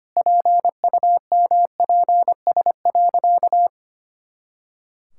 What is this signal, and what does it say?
Morse code, heard as a keyed tone, about 25 wpm, 700 Hz. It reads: PUMPS.
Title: Pangram Challenge v2 - 25wpm